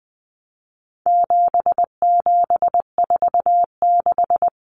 Morse code, heard as a keyed tone, about 19 wpm, 700 Hz. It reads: 7746
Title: Numbers - 4-Digit 20wpm